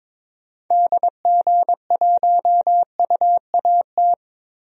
Morse code, heard as a keyed tone, about 21 wpm, 700 Hz. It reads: DG1UAT
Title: DXCC-100 Call Signs - 6 Characters - 22wpm